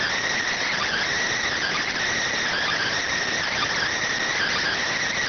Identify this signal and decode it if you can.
Emergency Alert System (SAME) tones (header bursts), heard as harsh buzzing bursts. It is heard from 0.5 to 4.8 s.